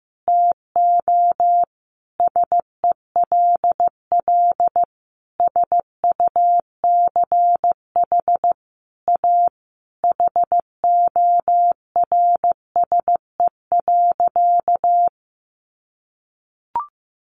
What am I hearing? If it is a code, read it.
Morse code, heard as a keyed tone, about 15 wpm, 700 Hz. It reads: TO SELL SUCH A HORSE.
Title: Sentences from Top 1100 Words 15wpm